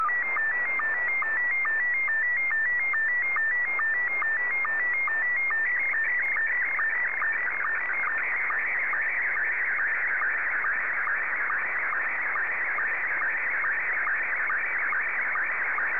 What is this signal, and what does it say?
SSTV (slow-scan television), Scottie 1.
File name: sstv / S1-VK3DEK